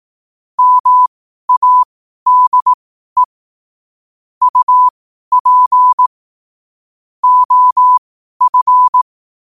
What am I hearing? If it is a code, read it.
Morse code, heard as a keyed tone, about 18 wpm, spaced as for 12 wpm, 1000 Hz. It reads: MADE UP OF
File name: qotdfiles / 12wpm